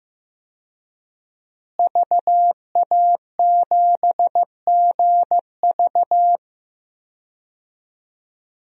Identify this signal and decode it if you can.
Morse code, heard as a keyed tone, about 15 wpm, 700 Hz. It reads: VA7GV
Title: DXCC-100 Call Signs - 5 Characters - Repeated 2x - 15wpm